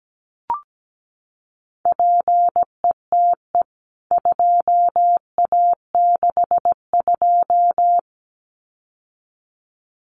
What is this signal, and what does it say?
Morse code, heard as a keyed tone, about 17 wpm, 700 Hz. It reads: PETE 2A62
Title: CWT Exchange 17wpm